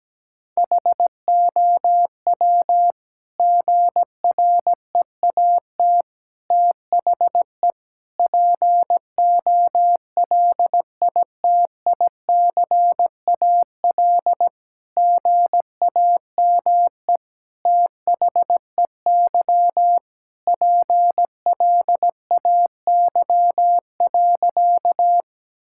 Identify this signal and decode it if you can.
Morse code, heard as a keyed tone, about 17 wpm, 700 Hz. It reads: HOW GREAT THE POLITICAL GAME THEY PLAY.
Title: Sentences from Top 1000 Words - Review Part 5 - 17wpm